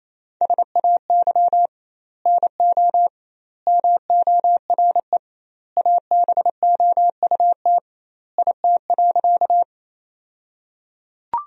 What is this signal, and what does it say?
Morse code, heard as a keyed tone, about 28 wpm, 700 Hz. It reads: SAY NO MORE ABOUT IT.
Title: 2x Word-Spacing - Sentences from Top 200 Words - 28wpm